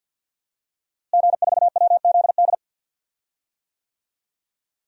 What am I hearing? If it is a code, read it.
Morse code, heard as a keyed tone, about 52 wpm, 700 Hz. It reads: G4WZD
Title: DXCC-100 Call Signs - 5 Characters - Repeated 2x - 50wpm